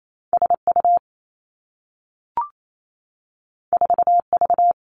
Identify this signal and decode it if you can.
Morse code, heard as a keyed tone, about 28 wpm, 700 Hz. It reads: S U 4 V
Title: Two Challenging Letter-Number Combinations 28wpm